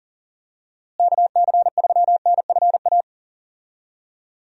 Morse code, heard as a keyed tone, about 40 wpm, 700 Hz. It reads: KC3NFA
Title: USA Call Signs - 6 Characters - Repeated 2x - 40wpm